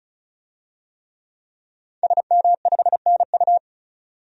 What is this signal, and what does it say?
Morse code, heard as a keyed tone, about 34 wpm, 700 Hz. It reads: SM5NU
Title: DXCC-100 Call Signs - 5 Characters - 35wpm